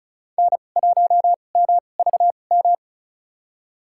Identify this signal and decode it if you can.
Morse code, heard as a keyed tone, about 34 wpm, spaced as for 25 wpm, 700 Hz. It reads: N1MVM
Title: USA Call Signs - 2x Character Spacing - 35wpm